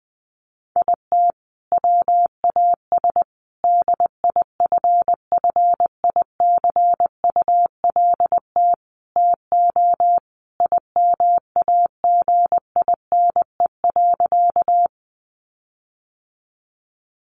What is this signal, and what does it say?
Morse code, heard as a keyed tone, about 20 wpm, 700 Hz. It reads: IT WAS DIFFICULT TO IMAGINE.
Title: Sentences from Top 1400 Words 20wpm